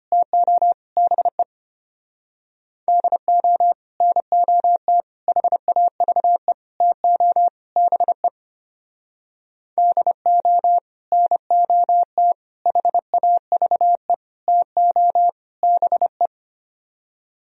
Morse code, heard as a keyed tone, about 33 wpm, 700 Hz. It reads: TO BE DO NOT HAVE TO BE DO NOT HAVE TO BE